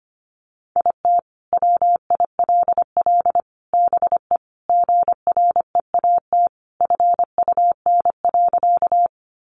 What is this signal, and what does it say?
Morse code, heard as a keyed tone, about 25 wpm, 700 Hz. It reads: IT WILL BE GREAT FUN.
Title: Sentences from Top 700 Words 25wpm